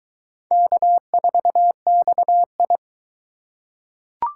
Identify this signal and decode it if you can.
Morse code, heard as a keyed tone, about 20 wpm, 700 Hz. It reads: K4XI